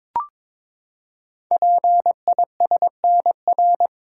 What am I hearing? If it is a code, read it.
Morse code, heard as a keyed tone, about 22 wpm, 700 Hz. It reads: PISNR
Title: Code Groups - 5 Letters - 22wpm